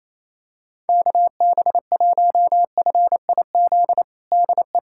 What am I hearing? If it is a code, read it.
Morse code, heard as a keyed tone, about 28 wpm, 700 Hz. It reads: KB1FIZ DE
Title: USA Callsign DE Callsign 28wpm